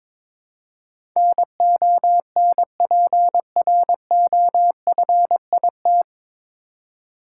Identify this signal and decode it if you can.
Morse code, heard as a keyed tone, about 22 wpm, 700 Hz. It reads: NONPROFIT
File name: 9 Letter Words - First 3 Letters Given - 22wpm